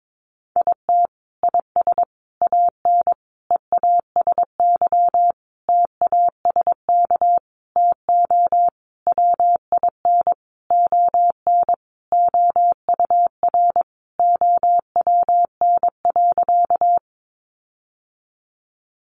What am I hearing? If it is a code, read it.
Morse code, heard as a keyed tone, about 22 wpm, 700 Hz. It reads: IT IS AN EASY TASK TO WIN ON OUR OWN.
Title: Sentences from Top 1500 Words 22wpm